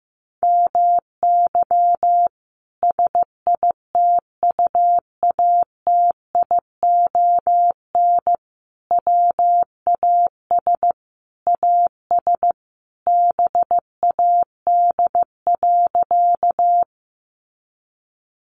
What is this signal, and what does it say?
Morse code, heard as a keyed tone, about 15 wpm, 700 Hz. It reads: MY SITUATION WAS AS BAD.
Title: Sentences from Top 1200 Words 15wpm